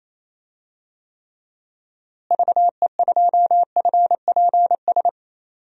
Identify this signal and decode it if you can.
Morse code, heard as a keyed tone, about 28 wpm, 700 Hz. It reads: VE2FPS
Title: DXCC-100 Call Signs - 6 Characters - Repeated 3x - 28wpm